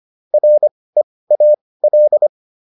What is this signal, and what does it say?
Morse code, heard as a keyed tone, about 25 wpm, spaced as for 18 wpm, 600 Hz. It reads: REAL
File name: PR305 18